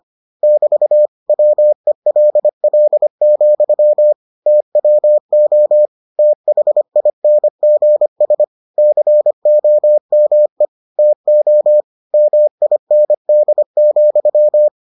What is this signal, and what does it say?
Morse code, heard as a keyed tone, about 25 wpm, 600 Hz. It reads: <BT> WELL, TWO THINGS COME TO MIND,